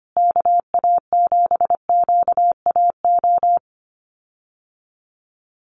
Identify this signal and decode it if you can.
Morse code, heard as a keyed tone, about 25 wpm, 700 Hz. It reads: KA7QAO
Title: USA Call Signs Repeated 2x 25wpm